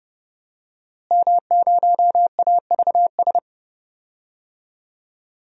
Morse code, heard as a keyed tone, about 30 wpm, 700 Hz. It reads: M0AVS
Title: DXCC-100 Call Signs - 5 Characters - Repeated 3x - 30wpm